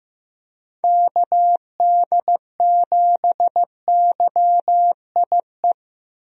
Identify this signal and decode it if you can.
Morse code, heard as a keyed tone, about 15 wpm, 700 Hz. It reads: KD7YIE
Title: USA Call Signs - 6 Characters - 15wpm